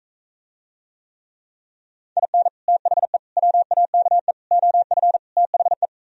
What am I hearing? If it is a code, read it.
Morse code, heard as a keyed tone, about 44 wpm, 700 Hz. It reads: IN THE WAKE OF THE